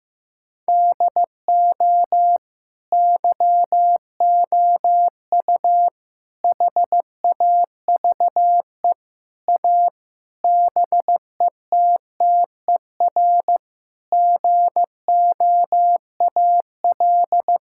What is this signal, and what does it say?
Morse code, heard as a keyed tone, about 15 wpm, 700 Hz. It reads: DO YOU HAVE A BETTER GOAL
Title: Sentences from Top 1000 Words - Review Part 5 - 15wpm